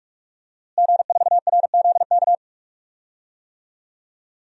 Morse code, heard as a keyed tone, about 43 wpm, 700 Hz. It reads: G4RZK
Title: DXCC-100 Call Signs - 5 Characters - Repeated 3x - 45wpm